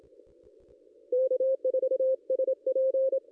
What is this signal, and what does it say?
Morse code, heard as a keyed tone, about 27 wpm, 520 Hz. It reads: K4SP
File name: contests / ss-cw-k5tr-op-n5rz -1900